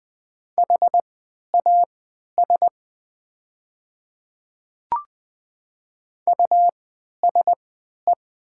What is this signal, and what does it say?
Morse code, heard as a keyed tone, about 20 wpm, spaced as for 11 wpm, 700 Hz. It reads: HAS USE
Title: 3x Character Spacing - Top 100 Words - 20wpm